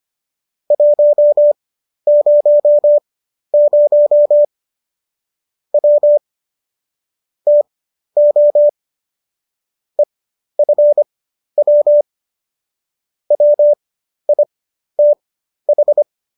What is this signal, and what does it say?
Morse code, heard as a keyed tone, about 25 wpm, spaced as for 12 wpm, 590 Hz. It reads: 100 W TO EFW WITH